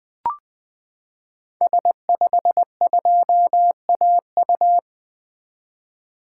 Morse code, heard as a keyed tone, about 20 wpm, 700 Hz. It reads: S52AU
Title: 2023 SOTA Activator Call Signs - 20wpm